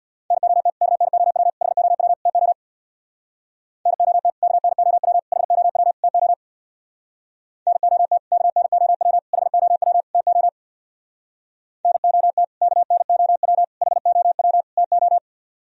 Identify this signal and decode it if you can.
Morse code, heard as a keyed tone, about 64 wpm, 700 Hz. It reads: NOT KNOW HOW TO NOT KNOW HOW TO NOT KNOW HOW TO NOT KNOW HOW TO